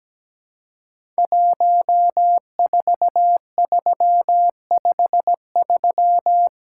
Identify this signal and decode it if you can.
Morse code, heard as a keyed tone, about 17 wpm, 700 Hz. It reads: 14353
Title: Code Groups - 5 Letter:Numbers - 17wpm